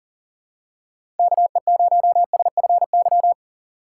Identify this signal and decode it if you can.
Morse code, heard as a keyed tone, about 37 wpm, 700 Hz. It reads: KE0SFY
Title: USA Call Signs - 6 Characters - 6x Character Spacing - 40wpm